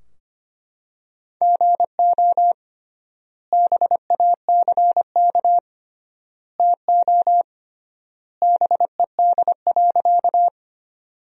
Morse code, heard as a keyed tone, about 25 wpm, 700 Hz. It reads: GO BACK TO BED.
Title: 3x Word-Spacing - Sentences from Top 500 Words - 25wpm